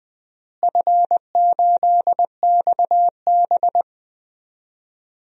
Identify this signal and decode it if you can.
Morse code, heard as a keyed tone, about 20 wpm, 700 Hz. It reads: F8XB
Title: DXCC-100 Call Signs - 4 Characters - Repeated 3x - 20wpm